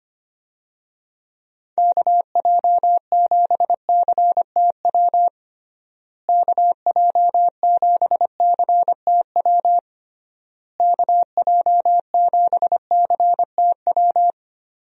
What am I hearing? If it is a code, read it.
Morse code, heard as a keyed tone, about 25 wpm, 700 Hz. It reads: KJ7CTW KJ7CTW KJ7CTW
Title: USA Call Signs - 6 Characters - Repeated 3x - 25wpm